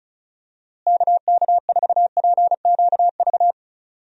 Morse code, heard as a keyed tone, about 36 wpm, 700 Hz. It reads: KK4PQV
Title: USA Call Signs - 6 Characters - Repeated 2x - 35wpm